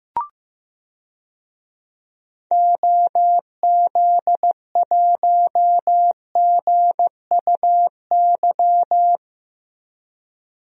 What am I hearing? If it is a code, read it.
Morse code, heard as a keyed tone, about 15 wpm, 700 Hz. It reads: OZ1GUY
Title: DXCC-100 Call Signs - 6 Characters - Repeated 3x - 15wpm